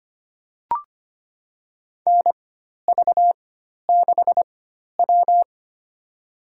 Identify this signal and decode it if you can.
Morse code, heard as a keyed tone, about 25 wpm, spaced as for 12 wpm, 690 Hz. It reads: NV6W
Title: USA Call Signs - 4 Characters - 4x Character Spacing - 25wpm